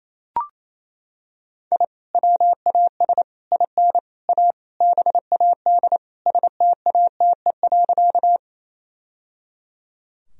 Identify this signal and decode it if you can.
Morse code, heard as a keyed tone, about 28 wpm, 700 Hz. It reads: I WAS IN A BAD STATE.
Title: Sentences from Top 1000 Words - Review Part 2 - 28wpm